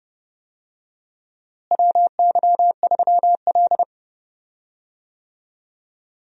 Morse code, heard as a keyed tone, about 30 wpm, 700 Hz. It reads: WY3L
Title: USA Call Signs - 4 Characters - Repeated 3x - 30wpm